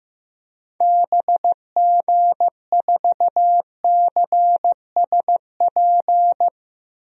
Morse code, heard as a keyed tone, about 15 wpm, 700 Hz. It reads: BG4CSP
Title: DXCC-100 Call Signs - 6 Characters - Repeated 3x - 15wpm